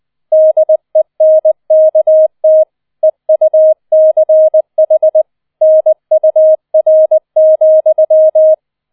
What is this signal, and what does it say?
Morse code, heard as a keyed tone, about 20 wpm, 630 Hz. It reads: DENKT EUCH NUR,
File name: 20wpm